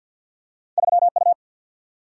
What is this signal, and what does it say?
Morse code, heard as a keyed tone, about 53 wpm, 700 Hz. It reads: 3U
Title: Single-Letter Single-Number 50wpm